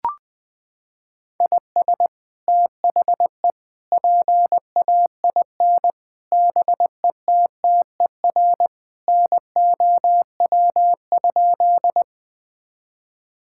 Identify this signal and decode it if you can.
Morse code, heard as a keyed tone, about 20 wpm, 700 Hz. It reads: IS THE PAIN BETTER NOW?
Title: Sentences from Top 1600 Words 20wpm